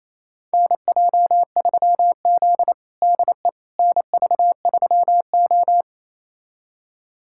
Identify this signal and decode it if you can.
Morse code, heard as a keyed tone, about 28 wpm, 700 Hz. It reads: NJ3Z DE NV3O
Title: USA Callsign DE Callsign - 4 Characters - 28wpm